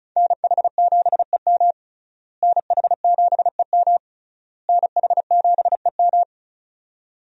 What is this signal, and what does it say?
Morse code, heard as a keyed tone, about 36 wpm, 700 Hz. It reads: NH7EM NH7EM NH7EM
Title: USA Call Signs - 5 Characters - Repeated 3x - 35wpm